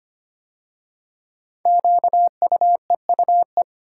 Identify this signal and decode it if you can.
Morse code, heard as a keyed tone, about 25 wpm, 700 Hz. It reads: QUEUE
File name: 5 Letter Words - First 3 Letters Given - 25wpm